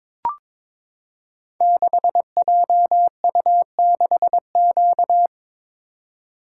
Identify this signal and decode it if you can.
Morse code, heard as a keyed tone, about 22 wpm, 700 Hz. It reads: 6JU6Q
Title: Code Groups - 5 Letter:Numbers - 22wpm